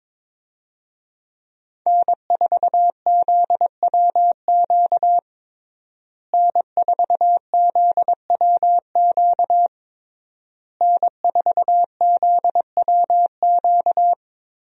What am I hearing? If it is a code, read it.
Morse code, heard as a keyed tone, about 22 wpm, 700 Hz. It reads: N4ZWQ N4ZWQ N4ZWQ
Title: USA Call Signs - 5 Characters - Repeated 3x - 22wpm